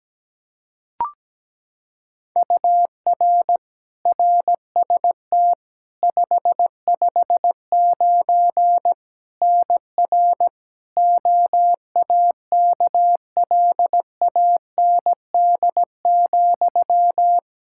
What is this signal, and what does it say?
Morse code, heard as a keyed tone, about 17 wpm, 700 Hz. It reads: UR RST 559 NR OAKLAND,
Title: K1USN SST QSO Practice 17wpm